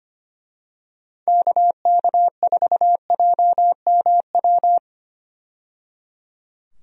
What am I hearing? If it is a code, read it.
Morse code, heard as a keyed tone, about 25 wpm, 700 Hz. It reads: KK4JMW
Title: USA Call Signs - 6 Characters - 25wpm